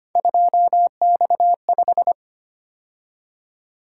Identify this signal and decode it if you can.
Morse code, heard as a keyed tone, about 25 wpm, 700 Hz. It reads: 2X5
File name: Three Letter-Number Combinations - 25wpm